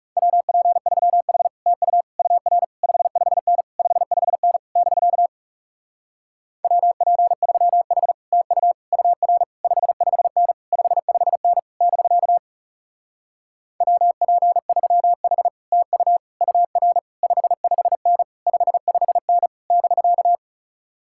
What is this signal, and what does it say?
Morse code, heard as a keyed tone, about 40 wpm, 700 Hz. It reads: WP3H TU UR 55N 55N <BK> WP3H TU UR 55N 55N <BK> WP3H TU UR 55N 55N <BK>